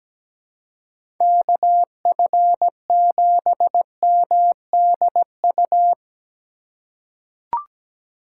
Morse code, heard as a keyed tone, about 17 wpm, 700 Hz. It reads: KF7MDU
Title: USA Call Signs - 6 Characters - 4x Character Spacing - 17wpm